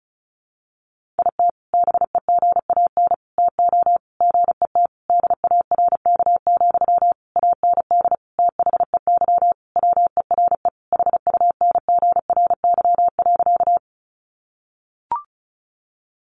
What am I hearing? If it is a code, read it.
Morse code, heard as a keyed tone, about 34 wpm, 700 Hz. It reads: IT BEGAN TO GET DARK, AND THEY WERE HUNGRY.
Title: Sentences from Top 1400 Words 35wpm